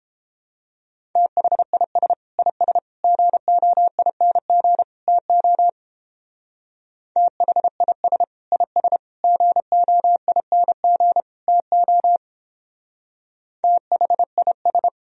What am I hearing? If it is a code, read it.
Morse code, heard as a keyed tone, about 28 wpm, 700 Hz. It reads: THIS IS GOING TO THIS IS GOING TO THIS